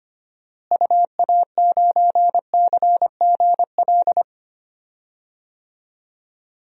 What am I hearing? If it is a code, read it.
Morse code, heard as a keyed tone, about 25 wpm, 700 Hz. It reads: UA9CGL
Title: DXCC-100 Call Signs - 6 Characters - Repeated 3x - 25wpm